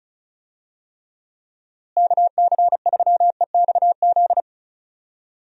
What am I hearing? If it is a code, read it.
Morse code, heard as a keyed tone, about 34 wpm, 700 Hz. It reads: KC3EXZ
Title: USA Call Signs - 6 Characters - Repeated 3x - 35wpm